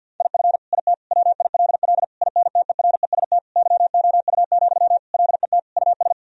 Morse code, heard as a keyed tone, about 50 wpm, 700 Hz. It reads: IF IT WILL INTEREST YOU, LET US
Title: Sentences from Top 1000 Words - Review Part 2 - 50wpm